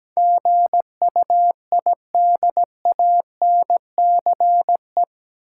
Morse code, heard as a keyed tone, about 17 wpm, 700 Hz. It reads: GUIDANCE
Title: Sentences from Top 1600 Words - New Word Drill - 17wpm